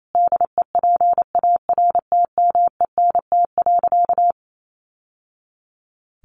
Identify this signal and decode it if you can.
Morse code, heard as a keyed tone, about 28 wpm, 700 Hz. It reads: DEPARTMENT.
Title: Sentences from Top 1200 Words 28wpm